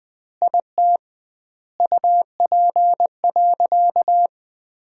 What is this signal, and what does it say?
Morse code, heard as a keyed tone, about 20 wpm, 700 Hz. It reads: IT UP.
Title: 2x Word-Spacing - Sentences from Top 100 Words - 20wpm